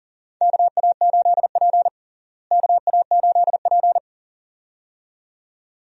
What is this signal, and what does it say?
Morse code, heard as a keyed tone, about 40 wpm, 700 Hz. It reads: KA8P KA8P
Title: USA Call Signs - 4 Characters - Repeated 2x - 40wpm